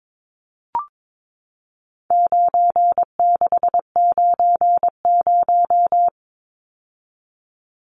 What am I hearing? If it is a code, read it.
Morse code, heard as a keyed tone, about 22 wpm, 700 Hz. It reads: 9690
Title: Numbers - 4-Digit 22wpm